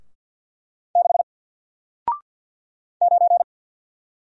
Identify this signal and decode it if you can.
Morse code, heard as a keyed tone, about 48 wpm, 700 Hz. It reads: 6 9
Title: Single Letter-Number Alt 50wpm